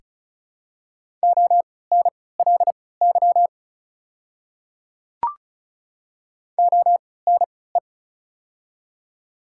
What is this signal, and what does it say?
Morse code, heard as a keyed tone, about 36 wpm, spaced as for 20 wpm, 700 Hz. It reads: ONLY ONE
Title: 3x Character Spacing - Top 200 Words - 35wpm